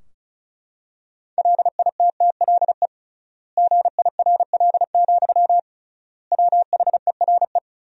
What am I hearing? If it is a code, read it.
Morse code, heard as a keyed tone, about 34 wpm, 700 Hz. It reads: LITTLE GIRL, WHERE
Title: 3x Word-Spacing - Sentences from Top 400 Words - 35wpm